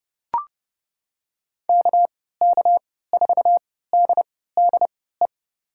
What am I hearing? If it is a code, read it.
Morse code, heard as a keyed tone, about 30 wpm, spaced as for 17 wpm, 700 Hz. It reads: KK4DDE
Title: USA Call Signs - 6 Characters - 3x Character Spacing - 30wpm